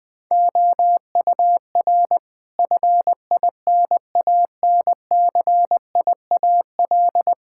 Morse code, heard as a keyed tone, about 20 wpm, 700 Hz. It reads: OUR FINANCIAL
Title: Sentences from Top 1000 Words - Review Part 5 - 20wpm